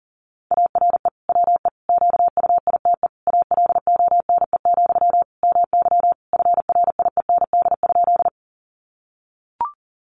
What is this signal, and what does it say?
Morse code, heard as a keyed tone, about 37 wpm, 700 Hz. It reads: ARE WE QUITE ALONE, MY FRIEND?
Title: Sentences from Top 1000 Words 40wpm